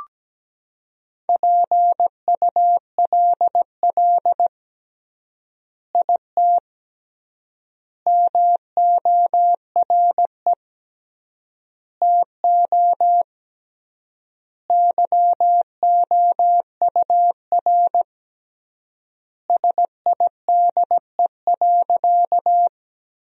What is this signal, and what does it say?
Morse code, heard as a keyed tone, about 17 wpm, 700 Hz. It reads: PULL IT MORE TO YOUR SIDE.
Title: 3x Word-Spacing - Sentences from Top 400 Words - 17wpm